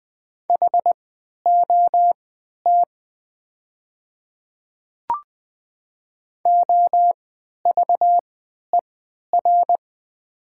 Morse code, heard as a keyed tone, about 20 wpm, spaced as for 11 wpm, 700 Hz. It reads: HOT OVER
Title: 3x Character Spacing - Top 300 Words - 20wpm